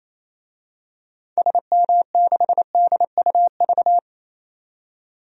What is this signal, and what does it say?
Morse code, heard as a keyed tone, about 28 wpm, 700 Hz. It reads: SM6DUV
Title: DXCC-100 Call Signs - 6 Characters - 28wpm